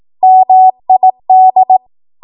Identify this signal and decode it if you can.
Morse code, heard as a keyed tone, about 18 wpm, 750 Hz. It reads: MID
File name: Morse / 18WPM